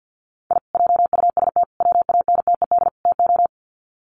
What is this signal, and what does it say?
Morse code, heard as a keyed tone, about 50 wpm, 700 Hz. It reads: I JUST WANTED TO
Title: Sets of 4 Words 50wpm